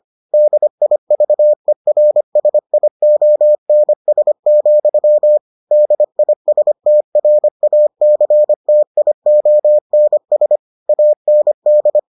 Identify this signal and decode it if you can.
Morse code, heard as a keyed tone, about 25 wpm, 600 Hz. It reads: DIVERSIONS, DISTRACTIONS AND